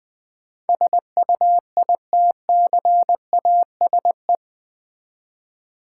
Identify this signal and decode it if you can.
Morse code, heard as a keyed tone, about 20 wpm, 700 Hz. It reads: SUITCASE
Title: Easy Words with Hints - 20wpm